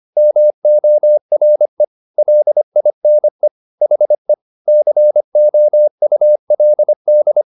Morse code, heard as a keyed tone, about 25 wpm, 600 Hz. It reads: MORE LINE HE COULD